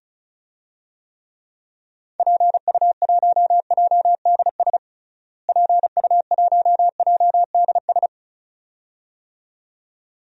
Morse code, heard as a keyed tone, about 34 wpm, 700 Hz. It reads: PU1JDS PU1JDS
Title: DXCC-100 Call Signs - 6 Characters - Repeated 2x - 35wpm